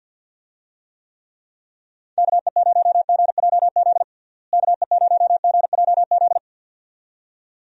Morse code, heard as a keyed tone, about 52 wpm, 700 Hz. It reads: KE0GJZ KE0GJZ
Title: USA Call Signs - 6 Characters - Repeated 2x - 50wpm